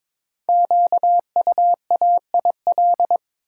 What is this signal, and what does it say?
Morse code, heard as a keyed tone, about 22 wpm, 690 Hz. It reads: QUAIL
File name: Difficult Words with Hints - 22wpm